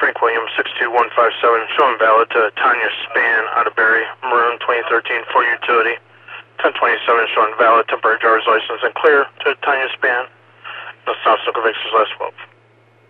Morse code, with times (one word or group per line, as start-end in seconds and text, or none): none